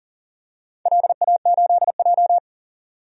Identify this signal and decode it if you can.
Morse code, heard as a keyed tone, about 40 wpm, 700 Hz. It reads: LA8J
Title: DXCC-100 Call Signs - 4 Characters - 40wpm